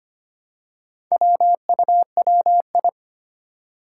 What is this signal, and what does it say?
Morse code, heard as a keyed tone, about 25 wpm, 700 Hz. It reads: WUWI
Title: Four Challenging Letter-Number Combinations 25wpm